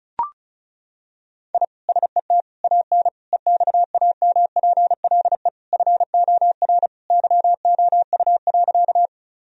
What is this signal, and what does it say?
Morse code, heard as a keyed tone, about 34 wpm, 700 Hz. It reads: I SET AN EXAMPLE FOR YOU.
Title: Sentences from Top 1000 Words - Review Part 1 - 35wpm